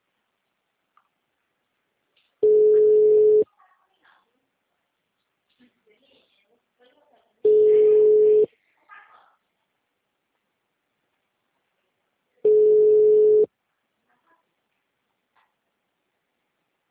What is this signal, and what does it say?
Ringback tone (Europe, 425 Hz), with speech.